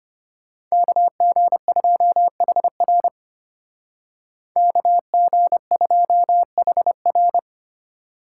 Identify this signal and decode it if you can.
Morse code, heard as a keyed tone, about 32 wpm, 700 Hz. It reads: KG2HR KG2HR